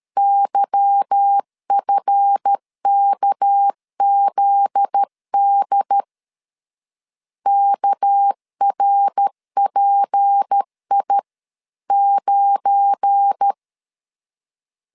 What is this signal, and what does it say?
Morse code, heard as a keyed tone, about 13 wpm, 800 Hz. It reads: YFKZD KRPI9
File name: LEZIONI / 8 YJ45 / 8 Y 12wpm 2228